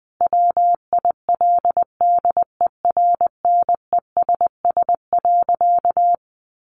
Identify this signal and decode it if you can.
Morse code, heard as a keyed tone, about 20 wpm, 700 Hz. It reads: WILDERNESS.
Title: A Sound Of Thunder 20wpm